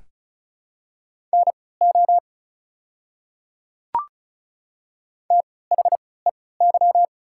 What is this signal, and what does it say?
Morse code, heard as a keyed tone, about 34 wpm, spaced as for 20 wpm, 700 Hz. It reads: NO THEY
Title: 3x Character Spacing - Top 200 Words - 35wpm